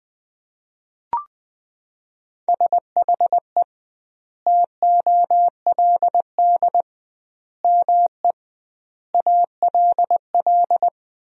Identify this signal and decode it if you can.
Morse code, heard as a keyed tone, about 20 wpm, 700 Hz. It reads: SHE TOLD ME ALL
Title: 2x Word-Spacing - Sentences from Top 400 Words - 20wpm